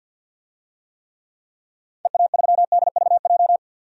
Morse code, heard as a keyed tone, about 52 wpm, 700 Hz. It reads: EA3DVJ
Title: DXCC-100 Call Signs - 6 Characters - 50wpm